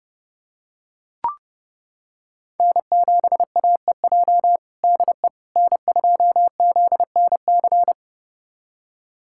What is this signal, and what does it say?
Morse code, heard as a keyed tone, about 30 wpm, 700 Hz. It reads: N7AEJ DE N2ZNC
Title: USA Callsign DE Callsign 30wpm